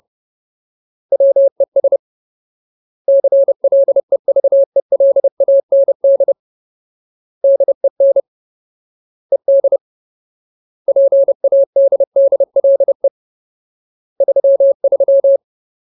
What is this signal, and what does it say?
Morse code, heard as a keyed tone, about 30 wpm, 550 Hz. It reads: WES CLEVELAND DEN ED PADDLE 33
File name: Ultimate Word List / 30WPM Ultimate